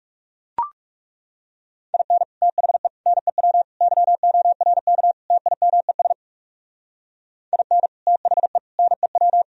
Morse code, heard as a keyed tone, about 37 wpm, 700 Hz. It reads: IN THE NEW YORK TIMES IN THE NEW
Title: Speed-Racing - Sets of 5 Words 30wpm